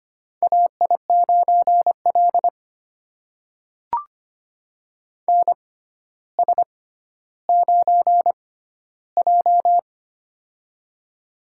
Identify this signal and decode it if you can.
Morse code, heard as a keyed tone, about 25 wpm, spaced as for 13 wpm, 700 Hz. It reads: AI9L NS9J
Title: USA Call Signs - 4 Characters - 6x Character Spacing - 25wpm